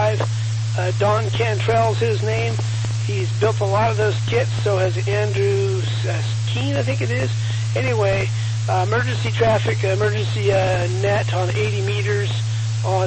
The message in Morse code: none

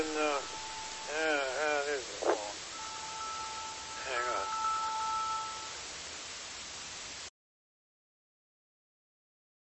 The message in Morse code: none

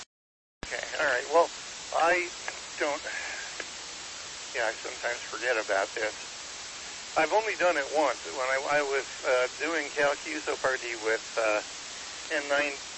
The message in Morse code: none

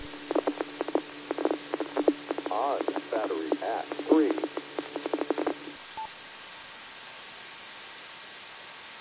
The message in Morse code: none

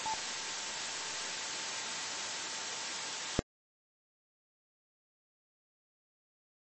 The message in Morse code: none

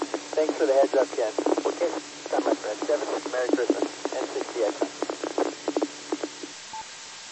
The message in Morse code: none

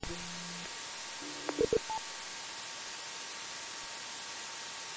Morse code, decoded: none

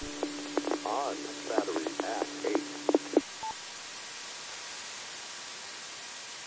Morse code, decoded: none

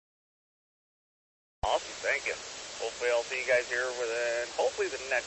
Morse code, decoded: none

